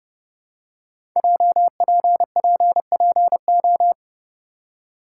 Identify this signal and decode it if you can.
Morse code, heard as a keyed tone, about 30 wpm, 700 Hz. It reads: JPPPO
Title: Code Groups - 5 Letters - 30wpm